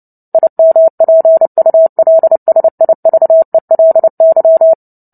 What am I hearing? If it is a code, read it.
Morse code, heard as a keyed tone, about 30 wpm, 650 Hz. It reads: IMPULSIVELY
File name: Book Two / 03-Chapter 3 - 29 WPM